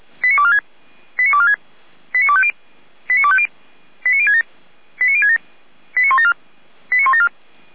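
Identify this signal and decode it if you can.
Five-tone selective call (ZVEI), heard as quick runs of five tones, tones of about 70 ms. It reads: 89246 89246 89250 89250 89067 89067 89174 89174
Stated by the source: ZVEI five-tone call received on SDR, from feuerwehr-orlamuende.org (fire brigade Orlamünde) audio archive.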